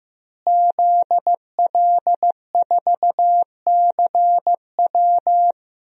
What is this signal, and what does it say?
Morse code, heard as a keyed tone, about 15 wpm, 700 Hz. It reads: ZL4CW
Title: DXCC-100 Call Signs - 5 Characters - Repeated 2x - 15wpm